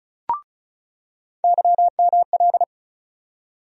Morse code, heard as a keyed tone, about 34 wpm, 700 Hz. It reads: YML
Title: USA Call Sign Postfixes - 3 Letters 35wpm